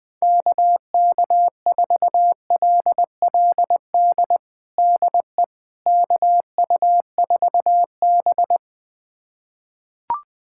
Morse code, heard as a keyed tone, about 20 wpm, 700 Hz. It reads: KK4LLD DE KU4B
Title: USA Callsign DE Callsign 20wpm